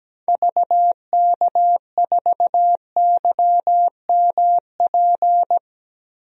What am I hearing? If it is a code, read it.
Morse code, heard as a keyed tone, about 17 wpm, 700 Hz. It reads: VK4YMP
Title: DXCC-100 Call Signs - 6 Characters - 17wpm